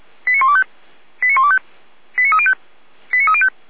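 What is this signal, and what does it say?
Five-tone selective call (ZVEI), heard as quick runs of five tones, tones of about 70 ms. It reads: 89125 89125 89305 89305
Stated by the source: ZVEI five-tone selective call 89310 received on an SDR.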